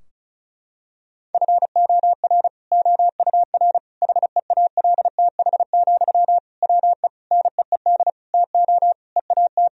Morse code, heard as a keyed tone, about 34 wpm, 700 Hz. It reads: FOR OUR HEALTH, WE NEED TO EAT
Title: Sentences from Top 1000 Words - Review Part 4 - 35wpm